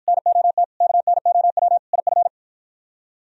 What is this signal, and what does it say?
Morse code, heard as a keyed tone, about 52 wpm, 700 Hz. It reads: NOT KNOW IF